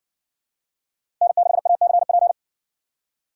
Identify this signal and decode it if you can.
Morse code, heard as a keyed tone, about 60 wpm, 700 Hz. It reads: N6ACP